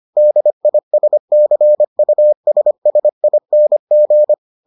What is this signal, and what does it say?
Morse code, heard as a keyed tone, about 25 wpm, 600 Hz. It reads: DISCUSSING